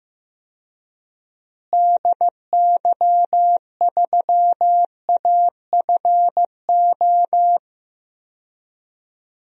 Morse code, heard as a keyed tone, about 15 wpm, 700 Hz. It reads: DY3AFO
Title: DXCC-100 Call Signs - 6 Characters - Repeated 2x - 15wpm